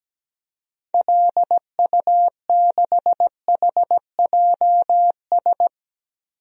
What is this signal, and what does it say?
Morse code, heard as a keyed tone, about 17 wpm, 700 Hz. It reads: LU6HJS
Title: DXCC-100 Call Signs - 6 Characters - Repeated 3x - 17wpm